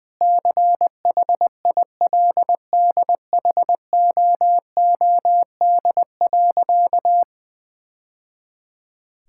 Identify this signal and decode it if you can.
Morse code, heard as a keyed tone, about 20 wpm, 700 Hz. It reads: CHILDHOOD.
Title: Sentences with 2-Word Phrases - 20wpm